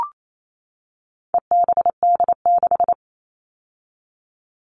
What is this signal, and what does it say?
Morse code, heard as a keyed tone, about 28 wpm, 700 Hz. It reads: EBD6
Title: All Cut Numbers - 4-Digit 28wpm